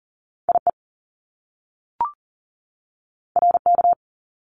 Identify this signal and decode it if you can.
Morse code, heard as a keyed tone, about 41 wpm, 700 Hz. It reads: IE RK
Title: Top Two Letter Combinations in Words 40wpm